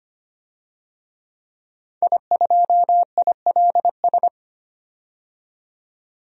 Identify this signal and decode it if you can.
Morse code, heard as a keyed tone, about 25 wpm, 700 Hz. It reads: I2ILS
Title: DXCC-100 Call Signs - 5 Characters - 25wpm